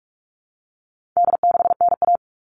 Morse code, heard as a keyed tone, about 44 wpm, 700 Hz. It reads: D6NA
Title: All Cut Numbers - 4-Digit 45wpm